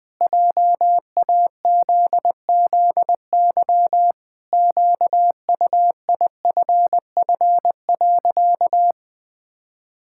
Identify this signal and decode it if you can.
Morse code, heard as a keyed tone, about 20 wpm, 700 Hz. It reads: JAZZY QUIFF.
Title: Pangram Challenge v2 - 20wpm